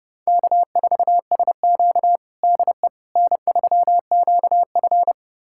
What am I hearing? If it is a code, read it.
Morse code, heard as a keyed tone, about 30 wpm, 700 Hz. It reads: K4SQ DE N3QF
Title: USA Callsign DE Callsign - 4 Characters - 30wpm